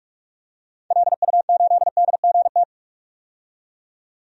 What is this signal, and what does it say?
Morse code, heard as a keyed tone, about 46 wpm, 700 Hz. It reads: LU8DGT
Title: DXCC-100 Call Signs - 6 Characters - 45wpm